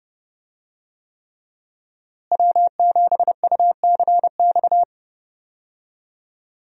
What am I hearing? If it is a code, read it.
Morse code, heard as a keyed tone, about 30 wpm, 700 Hz. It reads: W7UCX
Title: USA Call Signs - 5 Characters - Repeated 3x - 30wpm